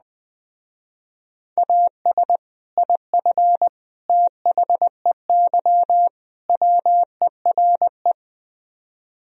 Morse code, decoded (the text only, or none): AS IF THEY WERE